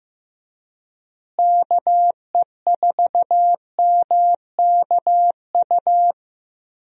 Morse code, heard as a keyed tone, about 15 wpm, 700 Hz. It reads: KE4MKU
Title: USA Call Signs - 6 Characters - Repeated 3x - 15wpm